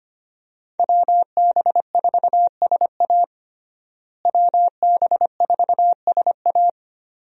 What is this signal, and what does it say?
Morse code, heard as a keyed tone, about 25 wpm, 700 Hz. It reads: WB4SA WB4SA
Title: USA Call Signs - 5 Characters - Repeated 2x - 25wpm